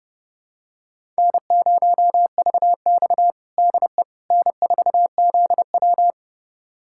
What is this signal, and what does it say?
Morse code, heard as a keyed tone, about 30 wpm, 700 Hz. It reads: N0VX DE N4ZW
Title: USA Callsign DE Callsign - 4 Characters - 30wpm